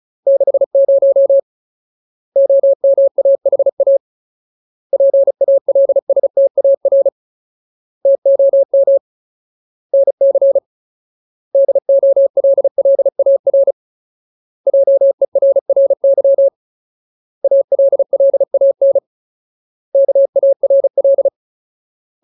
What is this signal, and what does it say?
Morse code, heard as a keyed tone, about 34 wpm, 550 Hz. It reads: 60 OMAHA PALSTAR TOM NC DOLLAR JERRY ALLAN KARL
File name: Ultimate Word List / 35WPM Ultimate